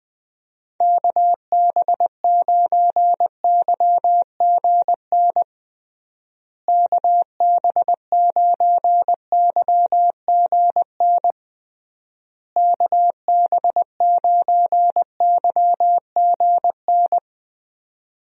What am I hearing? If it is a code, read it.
Morse code, heard as a keyed tone, about 21 wpm, 700 Hz. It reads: KB9YGN KB9YGN KB9YGN
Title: USA Call Signs - 6 Characters - Repeated 3x - 20wpm